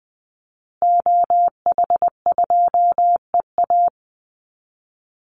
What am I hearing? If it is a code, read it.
Morse code, heard as a keyed tone, about 20 wpm, 700 Hz. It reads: OH2EA
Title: Call Signs Repeated 2x - 20wpm